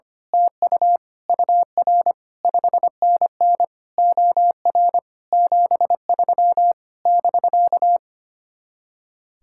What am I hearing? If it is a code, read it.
Morse code, heard as a keyed tone, about 24 wpm, 700 Hz. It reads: TU UR 5NN OR 73 <BK>
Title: Speed-Racing Common QSO Phrases - 25wpm